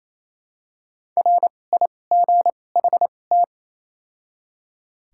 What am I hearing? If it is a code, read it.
Morse code, heard as a keyed tone, about 28 wpm, 700 Hz. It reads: R I G H T
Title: 2x Character Spacing - Top 200 Words - 28wpm